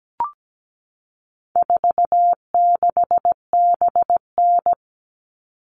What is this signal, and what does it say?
Morse code, heard as a keyed tone, about 17 wpm, 700 Hz. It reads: 46BN
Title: All Cut Numbers - 4-Digit 17wpm